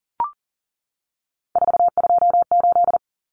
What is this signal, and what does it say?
Morse code, heard as a keyed tone, about 40 wpm, 700 Hz. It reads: 428
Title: Common Cut Numbers - 3-Digit 40wpm